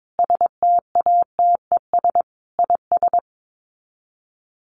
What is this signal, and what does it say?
Morse code, heard as a keyed tone, about 22 wpm, 700 Hz. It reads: STATES IS
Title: Sets of 5 Words 22wpm